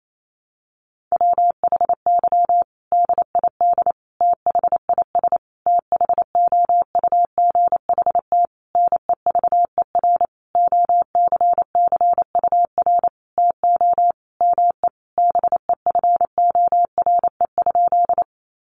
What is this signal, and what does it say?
Morse code, heard as a keyed tone, about 28 wpm, 700 Hz. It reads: WHY DID THIS THOUGHT NEVER OCCUR TO ME BEFORE?
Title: Sentences from Top 900 Words 28wpm